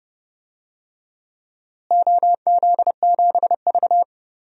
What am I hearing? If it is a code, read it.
Morse code, heard as a keyed tone, about 30 wpm, 700 Hz. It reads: OZ7V
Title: DXCC-100 Call Signs - 4 Characters - Repeated 3x - 30wpm